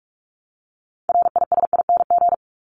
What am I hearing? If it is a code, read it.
Morse code, heard as a keyed tone, about 44 wpm, 700 Hz. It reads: RISING
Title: ING Suffix - Top 100 Words with Base Words 45wpm